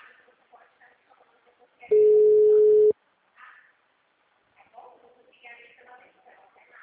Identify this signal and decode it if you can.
Ringback tone (Europe, 425 Hz), with speech.
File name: out- -2048- - - .3063